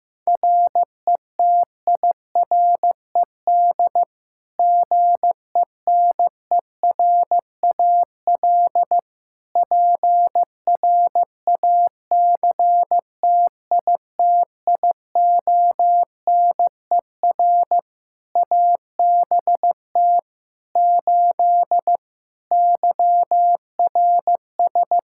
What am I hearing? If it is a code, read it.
Morse code, heard as a keyed tone, about 15 wpm, 700 Hz. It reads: RETIRED GENERAL PRACTITIONER ABT 8 YRS
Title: Practice QSO - Exchange 3 - More Info - 15wpm